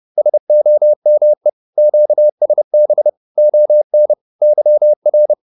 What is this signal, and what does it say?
Morse code, heard as a keyed tone, about 30 wpm, 600 Hz. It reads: SOME QSB ON YR